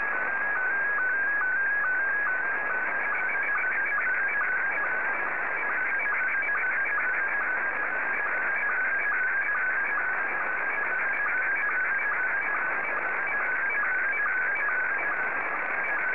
SSTV (slow-scan television), Scottie 1.